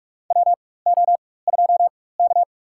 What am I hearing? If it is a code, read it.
Morse code, heard as a keyed tone, about 46 wpm, spaced as for 21 wpm, 700 Hz. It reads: WO2K
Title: USA Call Signs - 4 Characters - 4x Character Spacing - 45wpm